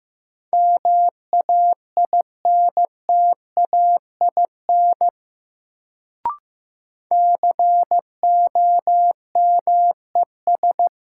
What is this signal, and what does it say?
Morse code, heard as a keyed tone, about 16 wpm, 700 Hz. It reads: MAINTAIN COMES
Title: Sentences from Top 1000 Words - Review Part 5 - New Word Drill - 15wpm